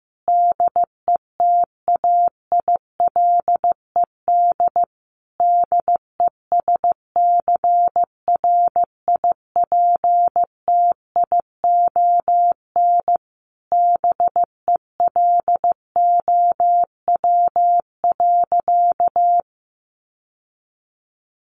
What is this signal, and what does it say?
Morse code, heard as a keyed tone, about 15 wpm, 700 Hz. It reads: DETAILED DESCRIPTION BELOW.
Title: Sentences from Top 1500 Words 15wpm